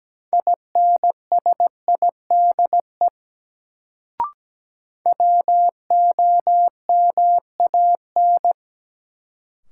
Morse code, heard as a keyed tone, about 17 wpm, 700 Hz. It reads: INSIDE WOMAN
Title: Sentences from Top 1000 Words - Review Part 2 - New Word Drill - 17wpm